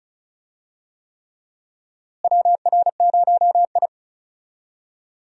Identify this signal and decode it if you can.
Morse code, heard as a keyed tone, about 34 wpm, 700 Hz. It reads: WR0I
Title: USA Call Signs - 4 Characters - Repeated 3x - 35wpm